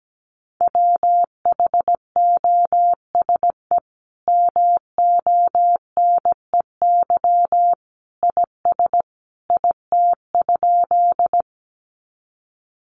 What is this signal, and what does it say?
Morse code, heard as a keyed tone, about 17 wpm, 700 Hz. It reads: WHOSE MONEY IS IT?
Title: Sentences from Top 1200 Words 17wpm